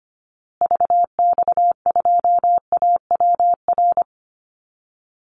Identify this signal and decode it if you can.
Morse code, heard as a keyed tone, about 25 wpm, 700 Hz. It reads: VX2AWR
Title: Call Signs Repeated 2x - 25wpm